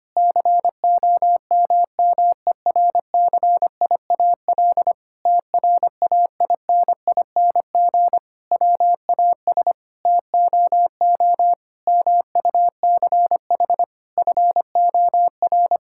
Morse code, heard as a keyed tone, about 25 wpm, 700 Hz. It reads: COMMERCIAL TRAINING WAS TOO MUCH FOR
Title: Sentences from Top 1000 Words - Review Part 5 - 25wpm